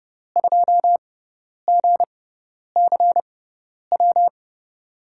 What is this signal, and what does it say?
Morse code, heard as a keyed tone, about 30 wpm, spaced as for 10 wpm, 700 Hz. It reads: 2GCW
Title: USA Call Signs - 6 Characters - 6x Character Spacing - 30wpm